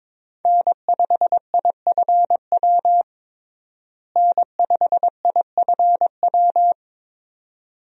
Morse code, heard as a keyed tone, about 22 wpm, 700 Hz. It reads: N5IFW N5IFW
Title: USA Call Signs - 5 Characters - Repeated 2x - 22wpm